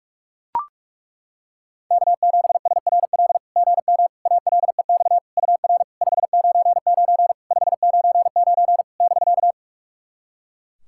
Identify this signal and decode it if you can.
Morse code, heard as a keyed tone, about 41 wpm, 690 Hz. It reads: K7SRL GM ALEX UR 599 599 <BK>
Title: SOTA QSO Practice 45wpm